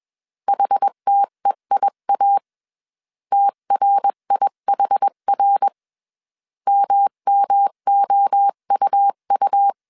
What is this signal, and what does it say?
Morse code, heard as a keyed tone, about 21 wpm, 800 Hz. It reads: HTEIA TRIHR MMOUU
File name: 20wpm